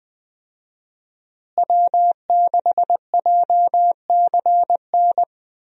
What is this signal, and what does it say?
Morse code, heard as a keyed tone, about 20 wpm, 700 Hz. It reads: W6JCN
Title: USA Call Signs - 5 Characters - Repeated 2x - 20wpm